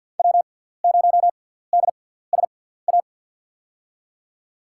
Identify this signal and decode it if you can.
Morse code, heard as a keyed tone, about 47 wpm, spaced as for 17 wpm, 700 Hz. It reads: W0DSA
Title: USA Call Signs - 5 Characters - 6x Character Spacing - 50wpm